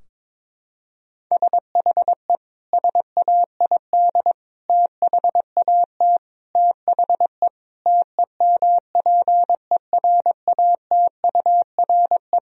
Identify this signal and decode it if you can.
Morse code, heard as a keyed tone, about 22 wpm, 700 Hz. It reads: SHE SAID THAT THE TEMPERATURE
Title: Sentences from Top 1000 Words - Review Part 4 - 22wpm